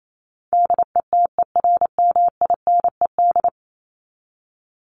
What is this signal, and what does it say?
Morse code, heard as a keyed tone, about 28 wpm, 700 Hz. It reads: DETERMINED
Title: ED Suffix - Top 100 Words with Base Words 28wpm